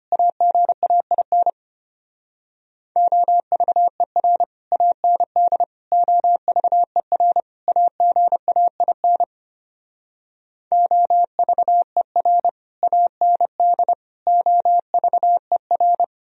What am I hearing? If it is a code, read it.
Morse code, heard as a keyed tone, about 33 wpm, 700 Hz. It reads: AGAIN OVER AND OVER AGAIN OVER AND OVER